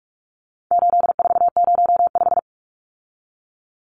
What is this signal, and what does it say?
Morse code, heard as a keyed tone, about 44 wpm, 700 Hz. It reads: 8405
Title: Numbers - 4-Digit 45wpm